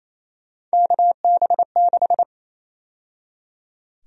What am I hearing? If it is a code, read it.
Morse code, heard as a keyed tone, about 28 wpm, 700 Hz. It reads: KB6
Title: USA Call Sign Prefixes - 28wpm